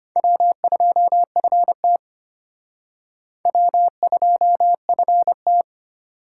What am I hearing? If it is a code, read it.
Morse code, heard as a keyed tone, about 33 wpm, 700 Hz. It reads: W2FT W2FT